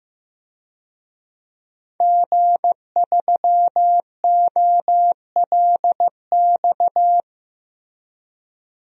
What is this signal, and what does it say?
Morse code, heard as a keyed tone, about 15 wpm, 700 Hz. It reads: G3OLX
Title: DXCC-100 Call Signs - 5 Characters - Repeated 3x - 15wpm